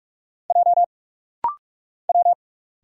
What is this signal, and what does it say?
Morse code, heard as a keyed tone, about 44 wpm, 700 Hz. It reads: J W
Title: Single Letter-Number Alt - Rapid-Fire 45wpm